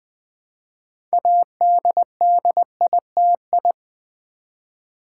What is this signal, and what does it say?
Morse code, heard as a keyed tone, about 20 wpm, 700 Hz. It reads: ADDITI
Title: Build A Word - 6 Letters At A Time - 12-Letter Words - 20wpm